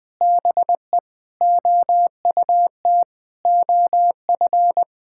Morse code, heard as a keyed tone, about 20 wpm, 700 Hz. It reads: BE OUT OF
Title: Sentences with 3-Word Phrases - 20wpm